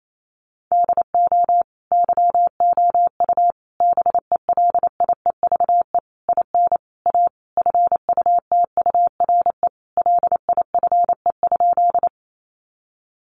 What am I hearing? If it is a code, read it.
Morse code, heard as a keyed tone, about 28 wpm, 700 Hz. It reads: DO YOU BELIEVE IN A FUTURE LIFE?
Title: Sentences from Top 1000 Words 28wpm